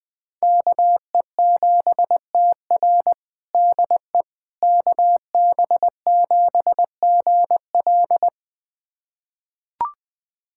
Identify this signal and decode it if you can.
Morse code, heard as a keyed tone, about 20 wpm, 700 Hz. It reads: KE7TR DE KB7GL
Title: USA Callsign DE Callsign - 5 Characters - 20wpm